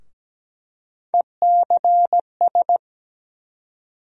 Morse code, heard as a keyed tone, about 17 wpm, 700 Hz. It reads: ECS
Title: USA Call Sign Postfixes - 3 Letters 17wpm